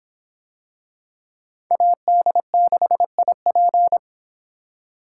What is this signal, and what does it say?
Morse code, heard as a keyed tone, about 23 wpm, 700 Hz. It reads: AD6IP